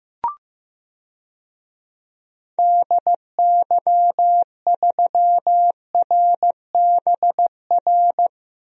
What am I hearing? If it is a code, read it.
Morse code, heard as a keyed tone, about 15 wpm, 700 Hz. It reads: DY3RBR
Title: DXCC-100 Call Signs - 6 Characters - Repeated 2x - 15wpm